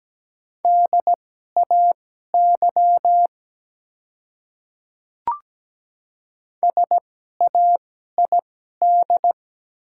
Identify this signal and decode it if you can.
Morse code, heard as a keyed tone, about 17 wpm, spaced as for 12 wpm, 700 Hz. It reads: DAY SAID
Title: 2x Character Spacing - Top 100 Words - 17wpm